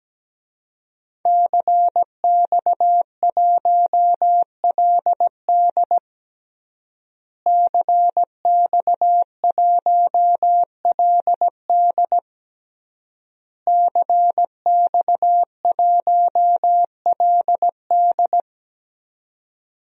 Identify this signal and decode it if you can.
Morse code, heard as a keyed tone, about 17 wpm, 700 Hz. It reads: CX1LD CX1LD CX1LD
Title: DXCC-100 Call Signs - 5 Characters - Repeated 3x - 17wpm